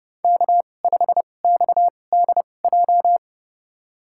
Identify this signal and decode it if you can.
Morse code, heard as a keyed tone, about 30 wpm, spaced as for 22 wpm, 700 Hz. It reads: K5XDJ
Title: USA Call Signs - 2x Character Spacing - 30wpm